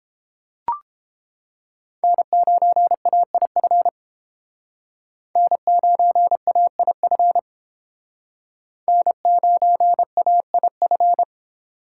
Morse code, heard as a keyed tone, about 31 wpm, 700 Hz. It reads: N9AIF N9AIF N9AIF